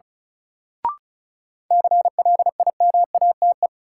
Morse code, heard as a keyed tone, about 34 wpm, 700 Hz. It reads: CLIMATE
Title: Sentences from Top 1600 Words - New Word Drill Alt - 35wpm